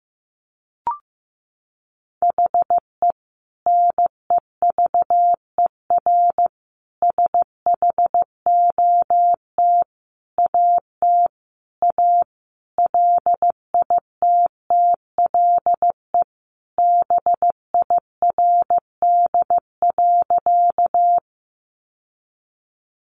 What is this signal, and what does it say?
Morse code, heard as a keyed tone, about 15 wpm, 700 Hz. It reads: HE NEVER SHOT AT A LITTLE BIRD.
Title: Sentences from Top 1100 Words 15wpm